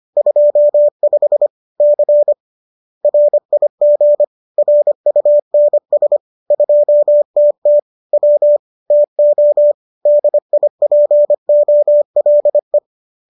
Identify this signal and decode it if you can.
Morse code, heard as a keyed tone, about 25 wpm, 590 Hz. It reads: 25 C RIG RUNS 2TT W TO DIPOLE